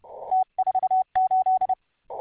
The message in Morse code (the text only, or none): T48